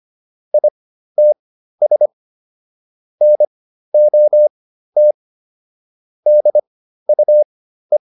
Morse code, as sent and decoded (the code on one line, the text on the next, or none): .. - ... / -. --- - / -.. ..- .
ITS NOT DUE